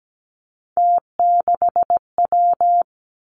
Morse code, heard as a keyed tone, about 17 wpm, 700 Hz. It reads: T6W
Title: Three Challenging Letter-Number Combinations 17wpm